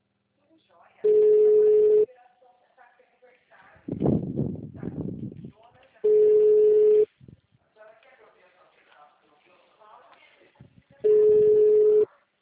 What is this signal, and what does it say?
Ringback tone (Europe, 425 Hz), with speech.